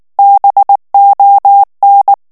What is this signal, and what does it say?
Morse code, heard as a keyed tone, about 19 wpm, 800 Hz. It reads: BON
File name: morse mp3 / FRANCAIS TEXTE CLAIR 1 VITESSE 18 MOTS MINUTES